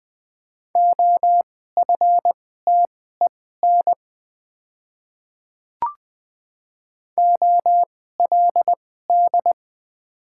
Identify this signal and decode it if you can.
Morse code, heard as a keyed tone, about 20 wpm, spaced as for 14 wpm, 700 Hz. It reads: OFTEN OLD
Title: 2x Character Spacing - Top 500 Words - 20wpm